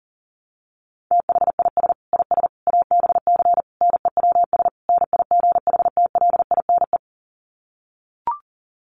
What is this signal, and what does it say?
Morse code, heard as a keyed tone, about 40 wpm, 700 Hz. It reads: THIS IS ABC NEWS NIGHTLINE
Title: Sets of 5 Words 40wpm